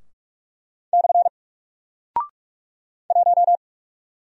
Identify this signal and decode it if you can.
Morse code, heard as a keyed tone, about 46 wpm, 700 Hz. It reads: / 1
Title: Single Character (Letter:Number:Punctuation:Prosign) Alt 45wpm